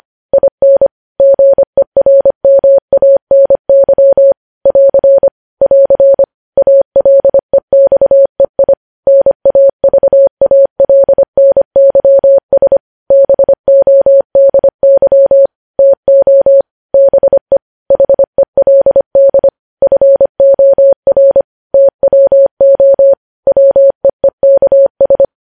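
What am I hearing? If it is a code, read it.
Morse code, heard as a keyed tone, about 25 wpm, 550 Hz. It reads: IN GERMANY <AR> <AR> ALEXEI NAVALNYS BODY TO BE HELD FOR TWO WEEKS
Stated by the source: Morse practice sending news headlines at 25 wpm.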